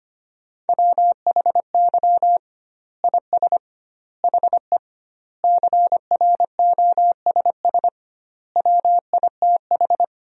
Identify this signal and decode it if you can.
Morse code, heard as a keyed tone, about 25 wpm, 700 Hz. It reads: WHY IS HE CROSS WITH
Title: 2x Word-Spacing - Sentences from Top 300 Words - 25wpm